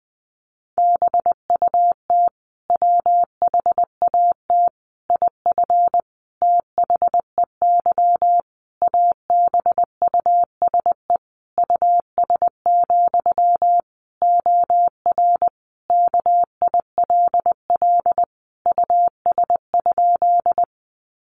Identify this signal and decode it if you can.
Morse code, heard as a keyed tone, about 19 wpm, 700 Hz. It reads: BUT WHAT IF THEY ABUSE US, OR KILL US?
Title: Sentences from Top 1400 Words 20wpm